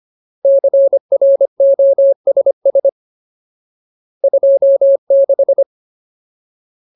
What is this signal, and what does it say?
Morse code, heard as a keyed tone, about 25 wpm, 550 Hz. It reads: CROSS 26
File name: Ultimate Word List / 25WPM Ultimate